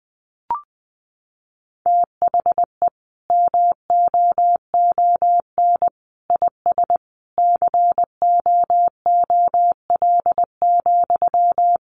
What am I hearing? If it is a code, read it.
Morse code, heard as a keyed tone, about 20 wpm, 700 Hz. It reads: THE MOON IS COOL,
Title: Sentences from Top 1200 Words 20wpm